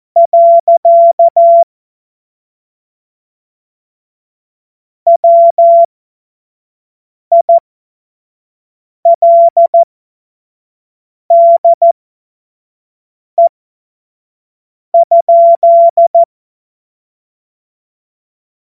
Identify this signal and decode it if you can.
Morse code, heard as a keyed tone, about 14 wpm, spaced as for 5 wpm, 680 Hz. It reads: . WILDE?
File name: Morsekurs / Witze / Witz 13. April 2021, 25 F70